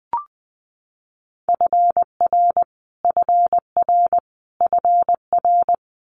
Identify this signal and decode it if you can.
Morse code, heard as a keyed tone, about 20 wpm, 700 Hz. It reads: FR FR FR
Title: Multi-letter Phonemes Repeated 3x 20wpm